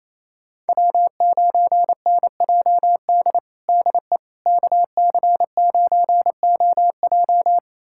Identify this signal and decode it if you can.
Morse code, heard as a keyed tone, about 28 wpm, 700 Hz. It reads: W9NJD DE KC9OJ
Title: USA Callsign DE Callsign - 5 Characters - 28wpm